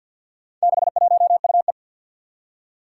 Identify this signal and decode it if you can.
Morse code, heard as a keyed tone, about 46 wpm, 700 Hz. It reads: 61UE
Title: Four Challenging Letter-Number Combinations 50wpm